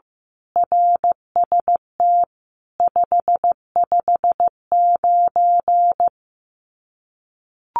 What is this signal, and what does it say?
Morse code, heard as a keyed tone, about 15 wpm, 700 Hz. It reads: RST 559
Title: Common QSO Phrases 15wpm